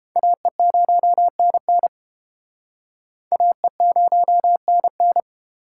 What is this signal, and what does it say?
Morse code, heard as a keyed tone, about 30 wpm, 700 Hz. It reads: AE0NN AE0NN